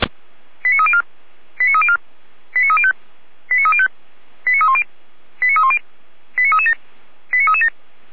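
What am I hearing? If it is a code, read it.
Five-tone selective call (ZVEI), heard as quick runs of five tones, tones of about 70 ms. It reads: 89304 89304 89305 89305 89310 89310 89337 89337
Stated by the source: ZVEI five-tone selective call 89250 received on an SDR by feuerwehr-orlamuende, a fire brigade at Orlamünde.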